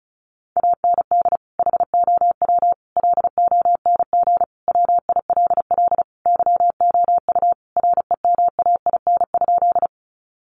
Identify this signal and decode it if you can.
Morse code, heard as a keyed tone, about 34 wpm, 700 Hz. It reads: AND HOW LONG WILL YOU REMAIN?
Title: Sentences from Top 1200 Words 35wpm